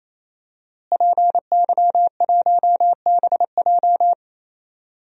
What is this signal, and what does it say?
Morse code, heard as a keyed tone, about 27 wpm, 700 Hz. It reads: PY1BJ
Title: DXCC-100 Call Signs - 5 Characters - Repeated 3x - 28wpm